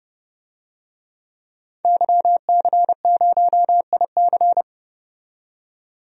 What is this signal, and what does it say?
Morse code, heard as a keyed tone, about 30 wpm, 700 Hz. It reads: YC0IC
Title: DXCC-100 Call Signs - 5 Characters - 30wpm